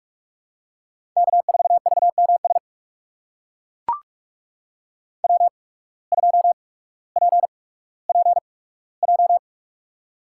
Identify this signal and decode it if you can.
Morse code, heard as a keyed tone, about 43 wpm, spaced as for 19 wpm, 700 Hz. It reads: K4VMS W2PPJ
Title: USA Call Signs - 5 Characters - 8x Character Spacing - 45wpm